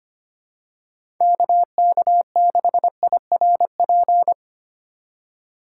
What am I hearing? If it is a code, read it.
Morse code, heard as a keyed tone, about 25 wpm, 700 Hz. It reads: KK6IRP
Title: USA Call Signs - 6 Characters - 4x Character Spacing - 25wpm